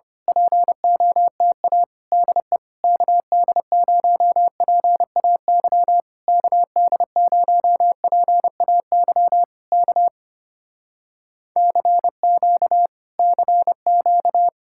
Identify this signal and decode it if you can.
Morse code, heard as a keyed tone, about 24 wpm, 700 Hz. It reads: POTA DE KD0PAY KD0PAY K CQ CQ
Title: Speed-Racing Common QSO Phrases - 25wpm